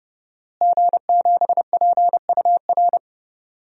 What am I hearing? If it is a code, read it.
Morse code, heard as a keyed tone, about 32 wpm, 700 Hz. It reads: G7PUR
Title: DXCC-100 Call Signs - 5 Characters - Repeated 3x - 30wpm